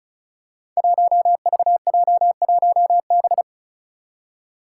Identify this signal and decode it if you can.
Morse code, heard as a keyed tone, about 36 wpm, 700 Hz. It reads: 1VJ1B
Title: Five Challenging Letter-Number Combinations 35wpm